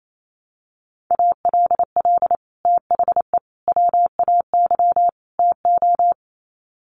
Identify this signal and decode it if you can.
Morse code, heard as a keyed tone, about 28 wpm, 700 Hz. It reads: ALL THE WAY TO
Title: Sets of 4 Words 28wpm